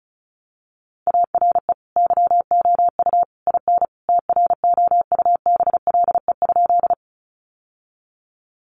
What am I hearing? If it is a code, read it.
Morse code, heard as a keyed tone, about 34 wpm, 700 Hz. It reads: ARE YOU IN TROUBLE?
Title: Sentences from Top 1200 Words 35wpm